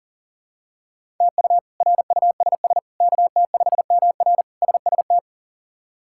Morse code, heard as a keyed tone, about 40 wpm, 690 Hz. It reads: TU RUSS KT5MR SST
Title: K1USN SST QSO Practice - 40wpm